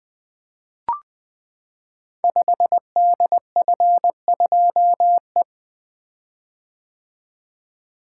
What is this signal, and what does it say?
Morse code, heard as a keyed tone, about 20 wpm, 700 Hz. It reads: 5DF2E
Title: Code Groups - 5 Letter:Numbers - 20wpm